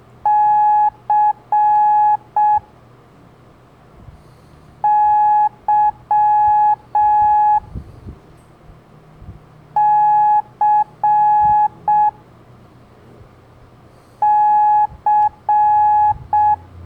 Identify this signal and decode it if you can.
Morse code, heard as a keyed tone, about 6 wpm, 840 Hz. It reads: C Y C C
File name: blog / LETRAS COM SOM SEMELHANTE MORSE